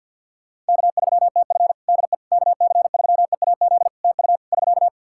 Morse code, heard as a keyed tone, about 50 wpm, 700 Hz. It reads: K3TF DE KC3EAZ TU <SK>
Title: Practice QSO - Exchange 4 - Ending - 50wpm